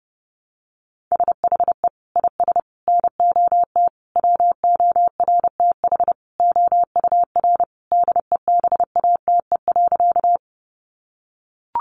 Morse code, heard as a keyed tone, about 30 wpm, 700 Hz. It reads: SHE IS NOT WORTH OUR DEBATE.
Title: Sentences from Top 1100 Words 30wpm